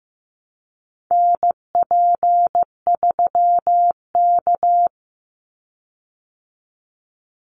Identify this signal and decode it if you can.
Morse code, heard as a keyed tone, about 15 wpm, 700 Hz. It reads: NP3K
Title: Call Signs Repeated 3x - 15wpm